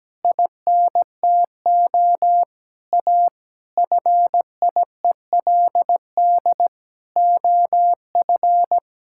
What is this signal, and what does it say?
Morse code, heard as a keyed tone, about 17 wpm, 700 Hz. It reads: INTO A FIELD OF
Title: Pangram Challenge v2 - 17wpm